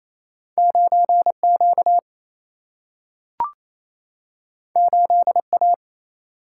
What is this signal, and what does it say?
Morse code, heard as a keyed tone, about 28 wpm, 700 Hz. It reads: 9 Q 8 A
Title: Single-Letter Single-Number 28wpm